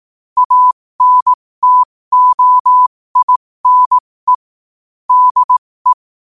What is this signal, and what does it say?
Morse code, heard as a keyed tone, about 18 wpm, spaced as for 15 wpm, 1000 Hz. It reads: ANTOINE DE
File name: qotdfiles / 15wpm